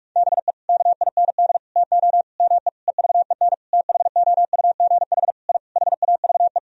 Morse code, heard as a keyed tone, about 46 wpm, 700 Hz. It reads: BE KIND TO ME EVEN THOUGH I HAVE
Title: Sentences from Top 1000 Words - Review Part 2 - 45wpm